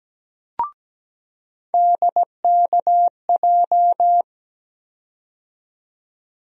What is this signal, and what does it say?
Morse code, heard as a keyed tone, about 17 wpm, 700 Hz. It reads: DKJ
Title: USA Call Sign Postfixes - 3 Letters 17wpm